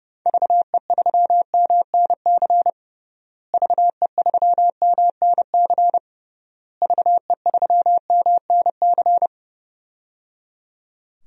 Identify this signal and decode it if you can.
Morse code, heard as a keyed tone, about 30 wpm, 700 Hz. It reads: VE3MNC VE3MNC VE3MNC
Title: DXCC-100 Call Signs - 6 Characters - Repeated 3x - 30wpm